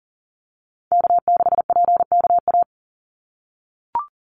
Morse code, heard as a keyed tone, about 40 wpm, 700 Hz. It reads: K6PKA
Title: USA Call Signs 40wpm